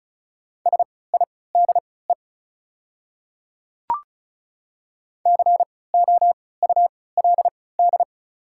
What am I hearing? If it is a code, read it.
Morse code, heard as a keyed tone, about 36 wpm, spaced as for 20 wpm, 700 Hz. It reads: SIDE COULD
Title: 3x Character Spacing - Top 100 Words - 35wpm